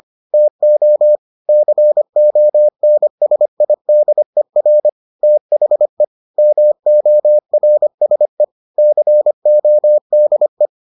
Morse code, heard as a keyed tone, about 25 wpm, 600 Hz. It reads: TO CONSIDER THE MORSE CODE